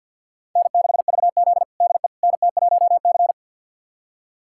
Morse code, heard as a keyed tone, about 47 wpm, 700 Hz. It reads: N6VZ DE NT1C
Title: USA Callsign DE Callsign - 4 Characters - 50wpm